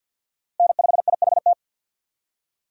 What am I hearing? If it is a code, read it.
Morse code, heard as a keyed tone, about 46 wpm, 700 Hz. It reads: N5IHT
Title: USA Call Signs - 5 Characters - Repeated 2x - 50wpm